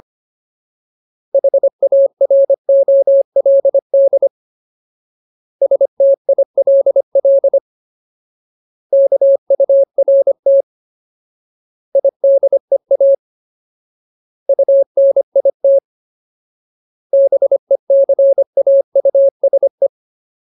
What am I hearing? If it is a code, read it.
Morse code, heard as a keyed tone, about 25 wpm, 550 Hz. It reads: HAROLD STILL KURT IDEA UNIT BECAUSE
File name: Ultimate Word List / 25WPM Ultimate